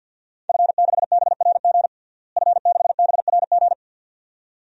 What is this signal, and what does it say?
Morse code, heard as a keyed tone, about 52 wpm, 700 Hz. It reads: F6BRG F6BRG
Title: DXCC-100 Call Signs - 5 Characters - Repeated 2x - 50wpm